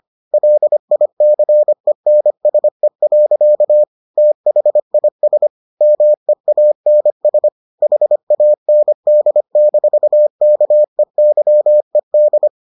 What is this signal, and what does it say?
Morse code, heard as a keyed tone, about 25 wpm, 600 Hz. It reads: LICENSE. THIS MEANS HAND-KEYED